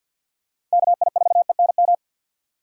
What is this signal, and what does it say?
Morse code, heard as a keyed tone, about 53 wpm, 700 Hz. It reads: KI4ENM
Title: USA Call Signs - 6 Characters - 3x Character Spacing - 50wpm